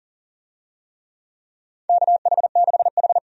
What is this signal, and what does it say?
Morse code, heard as a keyed tone, about 40 wpm, 700 Hz. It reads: KH6H
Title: USA Call Signs - 4 Characters - 40wpm